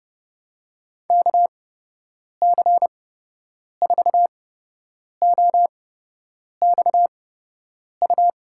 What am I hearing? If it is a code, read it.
Morse code, heard as a keyed tone, about 30 wpm, spaced as for 8 wpm, 700 Hz. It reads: KC4OXU
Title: USA Call Signs - 6 Characters - 8x Character Spacing - 30wpm